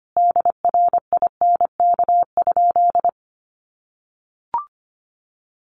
Morse code, heard as a keyed tone, about 25 wpm, 700 Hz. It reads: DRINK?
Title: Sentences from Top 1500 Words 25wpm